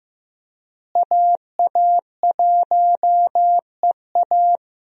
Morse code, heard as a keyed tone, about 15 wpm, 700 Hz. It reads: AA1EA
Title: USA Call Signs - 5 Characters - Repeated 3x - 15wpm